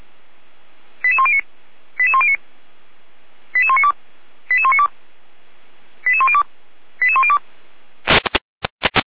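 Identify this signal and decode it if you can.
Five-tone selective call (ZVEI), heard as quick runs of five tones, tones of about 70 ms. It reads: 88109 88109 88192 88192 88102 88102
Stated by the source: ZVEI five-tone call received on SDR.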